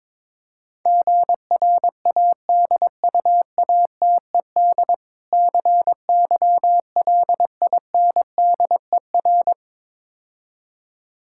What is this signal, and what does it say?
Morse code, heard as a keyed tone, about 22 wpm, 700 Hz. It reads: GRADUATED CYLINDER
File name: Easy Words - Speed-Racing with Hints - 22wpm